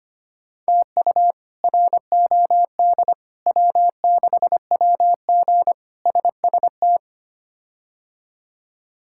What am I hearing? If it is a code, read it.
Morse code, heard as a keyed tone, about 25 wpm, 700 Hz. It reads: TU ROD W6WG SST
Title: K1USN SST QSO Practice - 25wpm